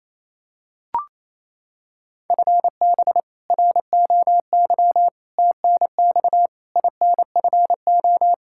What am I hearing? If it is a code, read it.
Morse code, heard as a keyed tone, about 28 wpm, 700 Hz. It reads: FB ROY TNX INFO
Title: K1USN SST QSO Practice 28wpm